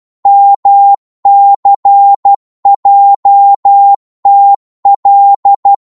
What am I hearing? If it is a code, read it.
Morse code, heard as a keyed tone, about 12 wpm, 800 Hz. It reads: MCJTL